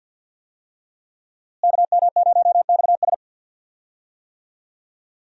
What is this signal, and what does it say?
Morse code, heard as a keyed tone, about 46 wpm, 700 Hz. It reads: KM0XS
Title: USA Call Signs - 5 Characters - Repeated 2x - 50wpm